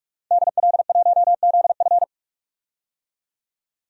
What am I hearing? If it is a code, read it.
Morse code, heard as a keyed tone, about 41 wpm, 700 Hz. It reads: DL1ZF
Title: DXCC-100 Call Signs - 5 Characters - 45wpm